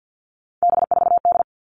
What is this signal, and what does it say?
Morse code, heard as a keyed tone, about 50 wpm, 700 Hz. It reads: B4D
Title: All Cut Numbers - 3-Digit 50wpm